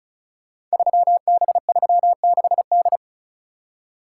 Morse code, heard as a keyed tone, about 36 wpm, 700 Hz. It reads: 3B36D
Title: Code Groups - 5 Letter:Numbers - 35wpm